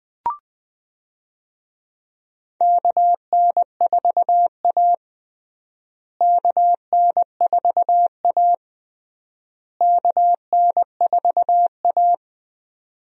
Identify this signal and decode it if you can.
Morse code, heard as a keyed tone, about 20 wpm, 700 Hz. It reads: KN4A KN4A KN4A
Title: USA Call Signs - 4 Characters - Repeated 3x - 20wpm